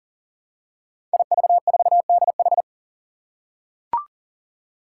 Morse code, heard as a keyed tone, about 40 wpm, 700 Hz. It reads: IV4DH
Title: Five Challenging Letter-Number Combinations 40wpm